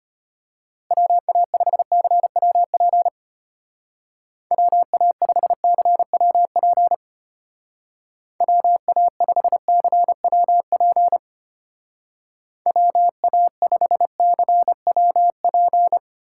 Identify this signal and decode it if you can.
Morse code, heard as a keyed tone, about 31 wpm, 700 Hz. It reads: WA5CWP WA5CWP WA5CWP WA5CWP